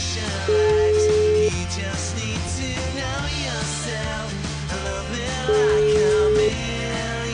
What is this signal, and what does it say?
Ringback tone (Europe, 425 Hz), with noise.